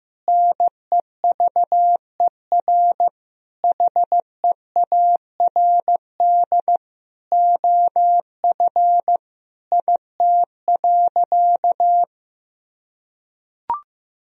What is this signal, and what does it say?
Morse code, heard as a keyed tone, about 15 wpm, 700 Hz. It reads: NEVER HEARD OF IT.
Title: Sentences from Top 1000 Words - Review Part 3 - 15wpm